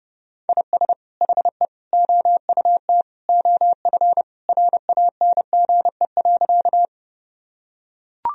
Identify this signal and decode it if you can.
Morse code, heard as a keyed tone, about 30 wpm, 700 Hz. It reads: IS HE OUT OF RANGE.
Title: Sentences from Top 1000 Words - Review Part 4 - 30wpm